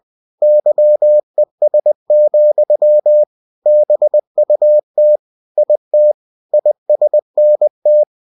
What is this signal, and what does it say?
Morse code, heard as a keyed tone, about 20 wpm, 600 Hz. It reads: YES, BUT IT ISNT
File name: cw-books / winnie-the-pooh / winnie-20wpm-page-0038